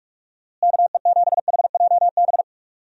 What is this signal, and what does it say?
Morse code, heard as a keyed tone, about 46 wpm, 700 Hz. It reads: KE7HJB
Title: USA Call Signs - 6 Characters - Repeated 2x - 45wpm